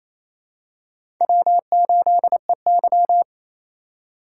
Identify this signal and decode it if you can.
Morse code, heard as a keyed tone, about 28 wpm, 700 Hz. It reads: W8EY
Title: USA Call Signs - 4 Characters - 4x Character Spacing - 28wpm